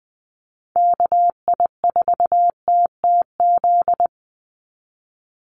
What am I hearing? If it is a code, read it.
Morse code, heard as a keyed tone, about 20 wpm, 700 Hz. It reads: KI4TTZ
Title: USA Call Signs 20wpm